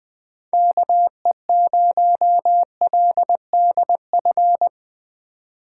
Morse code, heard as a keyed tone, about 23 wpm, 700 Hz. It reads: KE0LDF
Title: Speed Racing - USA 2x3 Call Signs 20wpm